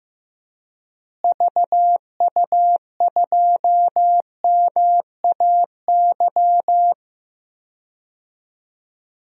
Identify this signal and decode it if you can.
Morse code, heard as a keyed tone, about 15 wpm, 700 Hz. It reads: VU2MAY
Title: DXCC-100 Call Signs - 6 Characters - Repeated 2x - 15wpm